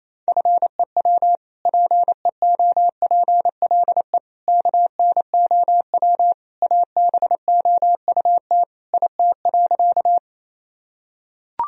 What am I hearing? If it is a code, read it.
Morse code, heard as a keyed tone, about 28 wpm, 700 Hz. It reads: FEW PEOPLE KNOW ABOUT IT.
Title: Sentences from Top 1000 Words - Review Part 1 - 28wpm